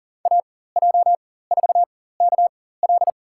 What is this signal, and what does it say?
Morse code, heard as a keyed tone, about 38 wpm, spaced as for 18 wpm, 700 Hz. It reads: AJ4KL
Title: USA Call Signs - 5 Characters - 4x Character Spacing - 40wpm